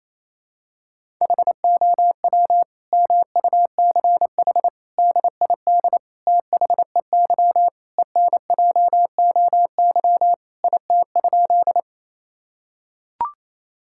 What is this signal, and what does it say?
Morse code, heard as a keyed tone, about 28 wpm, 700 Hz. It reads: HOW MUCH DID THEY ENJOY IT?
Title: Sentences from Top 1000 Words - Review Part 5 - 28wpm